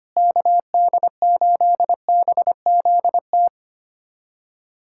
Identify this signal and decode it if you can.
Morse code, heard as a keyed tone, about 25 wpm, 700 Hz. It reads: KD8BZT
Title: USA Callsign DE Callsign 25wpm